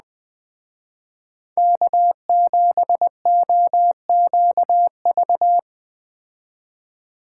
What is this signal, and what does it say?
Morse code, heard as a keyed tone, about 20 wpm, 700 Hz. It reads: K7OQV